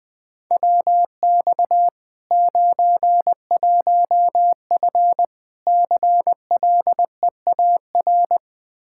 Morse code, heard as a keyed tone, about 20 wpm, 700 Hz. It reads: WX 91F CLEAR
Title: K1USN SST QSO Practice 20wpm